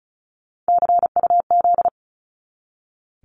Morse code, heard as a keyed tone, about 34 wpm, 700 Hz. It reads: CUZ
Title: Abbreviations 35wpm